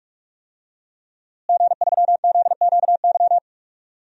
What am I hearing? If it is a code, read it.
Morse code, heard as a keyed tone, about 46 wpm, 700 Hz. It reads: G3ZQY
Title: DXCC-100 Call Signs - 5 Characters - Repeated 3x - 45wpm